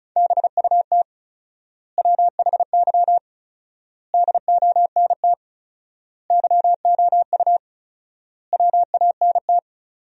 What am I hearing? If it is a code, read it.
Morse code, heard as a keyed tone, about 34 wpm, 700 Hz. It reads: BUT WHY DONT YOU WANT
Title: 4x Word-Spacing - Sentences from Top 300 Words - 35wpm